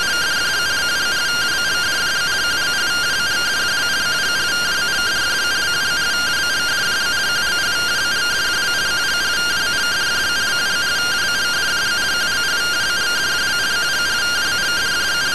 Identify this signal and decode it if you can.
Two-tone teleprinter signal, about 170 Hz shift.
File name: findings / SAGITTAL 43mm 4mm.gen-17 rtty rx